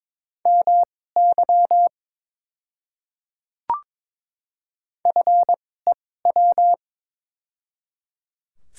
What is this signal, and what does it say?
Morse code, heard as a keyed tone, about 22 wpm, spaced as for 16 wpm, 700 Hz. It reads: MY FEW
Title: 2x Character Spacing - Top 300 Words - 22wpm